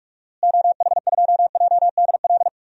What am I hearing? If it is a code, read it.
Morse code, heard as a keyed tone, about 46 wpm, 700 Hz. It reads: OH2JDL
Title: DXCC-100 Call Signs - 6 Characters - Repeated 2x - 45wpm